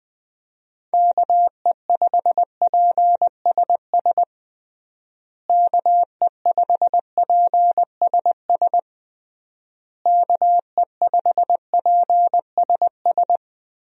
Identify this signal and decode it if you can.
Morse code, heard as a keyed tone, about 21 wpm, 700 Hz. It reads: KE5PSS KE5PSS KE5PSS
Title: USA Call Signs - 6 Characters - Repeated 3x - 20wpm